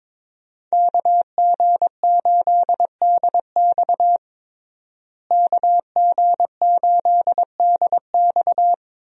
Morse code, heard as a keyed tone, about 22 wpm, 700 Hz. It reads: KG8DX KG8DX
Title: USA Call Signs - 5 Characters - Repeated 2x - 22wpm